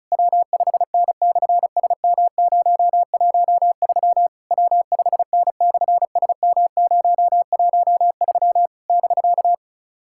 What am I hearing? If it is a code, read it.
Morse code, heard as a keyed tone, about 38 wpm, 700 Hz. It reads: W5N/SM013 W5N/SM013 <BK>
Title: SOTA QSO Practice 35wpm